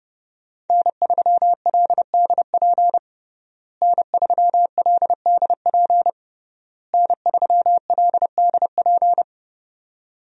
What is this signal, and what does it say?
Morse code, heard as a keyed tone, about 30 wpm, 700 Hz. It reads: N3LDP N3LDP N3LDP
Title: USA Call Signs - 5 Characters - Repeated 3x - 30wpm